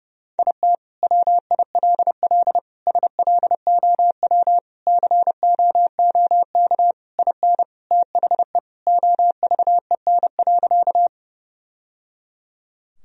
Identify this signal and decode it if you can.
Morse code, heard as a keyed tone, about 30 wpm, 700 Hz. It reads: IT WILL SLOW COOK IN THE OVEN.
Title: Sentences from Top 1000 Words - Review Part 4 - 30wpm